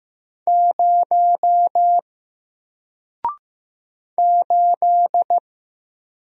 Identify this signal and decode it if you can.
Morse code, heard as a keyed tone, about 15 wpm, 700 Hz. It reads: 0 8
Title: Single Letter-Number Alt 15wpm